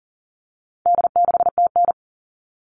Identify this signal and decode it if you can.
Morse code, heard as a keyed tone, about 40 wpm, 700 Hz. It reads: D6TN
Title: All Cut Numbers - 4-Digit 40wpm